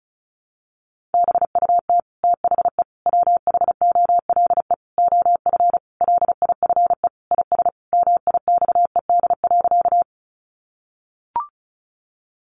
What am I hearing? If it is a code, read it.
Morse code, heard as a keyed tone, about 34 wpm, 700 Hz. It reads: BUT THE WHOLE OF LIFE IS MIXED.
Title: Sentences from Top 1500 Words 35wpm